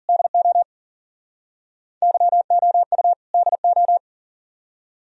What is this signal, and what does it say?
Morse code, heard as a keyed tone, about 54 wpm, 690 Hz. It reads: DO YOU DO